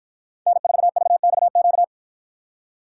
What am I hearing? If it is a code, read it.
Morse code, heard as a keyed tone, about 51 wpm, 700 Hz. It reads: N4VXQ